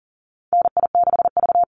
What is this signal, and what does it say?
Morse code, heard as a keyed tone, about 41 wpm, 700 Hz. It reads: NI6V
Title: Call Signs - 40wpm